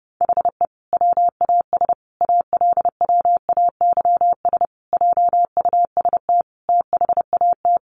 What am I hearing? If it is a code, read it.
Morse code, heard as a keyed tone, about 30 wpm, 700 Hz. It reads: HE WAS ALWAYS JUST THAT
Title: Sentences from Top 600 Words 30wpm